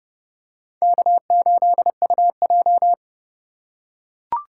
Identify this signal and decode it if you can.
Morse code, heard as a keyed tone, about 29 wpm, 700 Hz. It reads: K8UJ